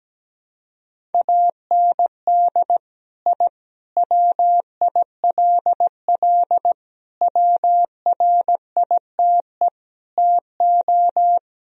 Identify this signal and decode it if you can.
Morse code, heard as a keyed tone, about 17 wpm, 700 Hz. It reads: AND I WILL WRITE TO
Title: Sentences from Top 1000 Words - Review Part 1 - 17wpm